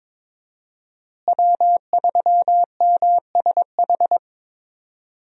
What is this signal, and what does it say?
Morse code, heard as a keyed tone, about 22 wpm, 700 Hz. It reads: W3MSH
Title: USA Call Signs - 5 Characters - Repeated 3x - 22wpm